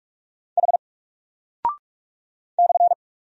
Morse code, heard as a keyed tone, about 44 wpm, 700 Hz. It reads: H /
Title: Single Character (Letter:Number:Punctuation:Prosign) Alt 45wpm